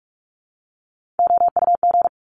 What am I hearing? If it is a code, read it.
Morse code, heard as a keyed tone, about 46 wpm, 700 Hz. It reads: OUG
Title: Top Three Letter Combinations in Words 45wpm